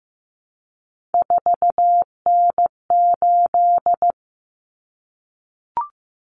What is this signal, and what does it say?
Morse code, heard as a keyed tone, about 15 wpm, 700 Hz. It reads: 4N8
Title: Common Cut Numbers - 3-Digit 15wpm